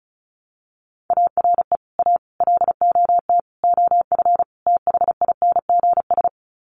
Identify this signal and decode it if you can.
Morse code, heard as a keyed tone, about 34 wpm, 700 Hz. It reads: ARE A LOT OF THINGS
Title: Sets of 5 Words Encore2 35wpm